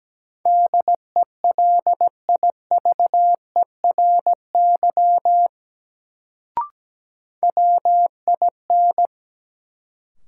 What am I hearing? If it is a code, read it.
Morse code, heard as a keyed tone, about 17 wpm, 700 Hz. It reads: DELIVERY WIN
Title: Sentences from Top 1500 Words - New Word Drill - 17wpm